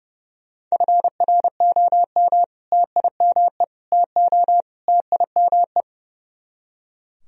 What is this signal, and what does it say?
Morse code, heard as a keyed tone, about 37 wpm, 700 Hz. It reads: FROM TIME TO TIME
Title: Speed-Racing - Sets of 4 Words 30wpm